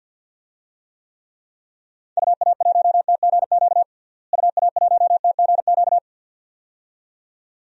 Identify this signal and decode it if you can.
Morse code, heard as a keyed tone, about 53 wpm, 700 Hz. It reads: UA1TGQ UA1TGQ
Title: DXCC-100 Call Signs - 6 Characters - Repeated 2x - 50wpm